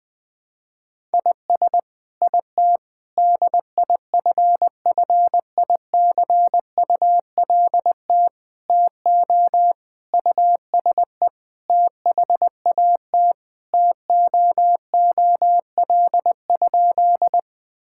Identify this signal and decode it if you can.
Morse code, heard as a keyed tone, about 20 wpm, 700 Hz. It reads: IS IT DIFFICULT TO USE THAT TOOL?
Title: Sentences from Top 1000 Words - Review Part 6 - 20wpm